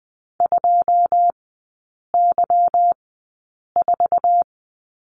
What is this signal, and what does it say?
Morse code, heard as a keyed tone, about 20 wpm, spaced as for 8 wpm, 700 Hz. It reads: 2Y4
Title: 145 Lesson 24e - Letter y - Three Characters 20wpm